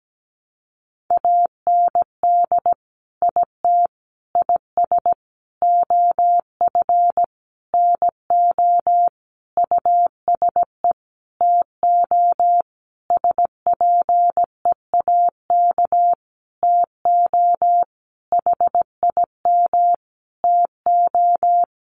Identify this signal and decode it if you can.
Morse code, heard as a keyed tone, about 17 wpm, 700 Hz. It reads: AND IT IS OF NO USE TO SPEAK TO HIM TO
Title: Sentences from Top 1100 Words 17wpm